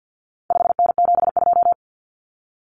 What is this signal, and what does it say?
Morse code, heard as a keyed tone, about 50 wpm, 700 Hz. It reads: 5N72
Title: Common Cut Numbers - 4-Digit 50wpm